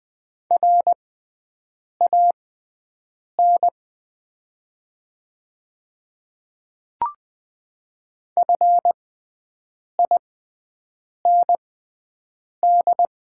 Morse code, heard as a keyed tone, about 20 wpm, spaced as for 7 wpm, 690 Hz. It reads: RAN FIND
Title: 6x Character Spacing - Top 500 Words - 20wpm